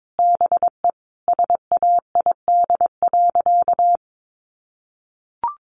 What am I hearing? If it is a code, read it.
Morse code, heard as a keyed tone, about 22 wpm, 700 Hz. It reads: BE SAID.
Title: Sentences from Top 100 Words 22wpm